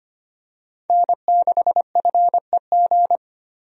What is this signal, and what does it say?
Morse code, heard as a keyed tone, about 25 wpm, 690 Hz. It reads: N6FEG
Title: USA Call Signs - 5 Characters - 25wpm